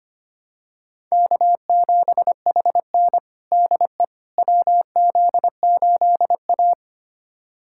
Morse code, heard as a keyed tone, about 25 wpm, 700 Hz. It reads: K7HN DE WZ8A
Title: USA Callsign DE Callsign - 4 Characters - 25wpm